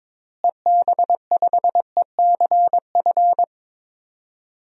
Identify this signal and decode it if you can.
Morse code, heard as a keyed tone, about 22 wpm, 700 Hz. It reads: EB5ECF
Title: DXCC-100 Call Signs - 6 Characters - Repeated 2x - 22wpm